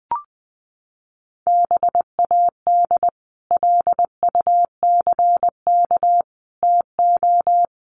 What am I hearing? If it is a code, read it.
Morse code, heard as a keyed tone, about 20 wpm, 700 Hz. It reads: BAD LUCK TO
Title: Sentences from Top 1500 Words 20wpm